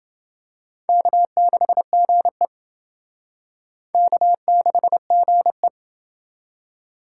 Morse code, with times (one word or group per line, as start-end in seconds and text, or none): 0.9-2.5 K6GE
3.9-5.7 K6GE